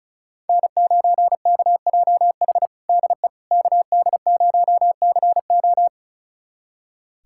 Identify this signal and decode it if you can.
Morse code, heard as a keyed tone, about 36 wpm, 700 Hz. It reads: N9KJH DE KD0CO
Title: USA Callsign DE Callsign - 5 Characters - 35wpm